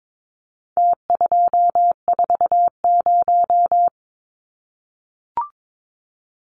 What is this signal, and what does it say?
Morse code, heard as a keyed tone, about 22 wpm, 700 Hz. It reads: T240
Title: Numbers - Contest Style - 4-Digit 22wpm